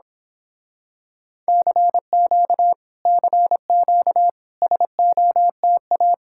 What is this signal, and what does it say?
Morse code, heard as a keyed tone, about 26 wpm, 700 Hz. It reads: CQ CQ SOTA
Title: Speed-Racing Common QSO Phrases - 22wpm